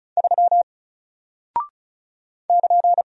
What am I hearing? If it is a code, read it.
Morse code, heard as a keyed tone, about 36 wpm, 700 Hz. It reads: 3 <KN>
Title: Single Character (Letter:Number:Punctuation:Prosign) 35wpm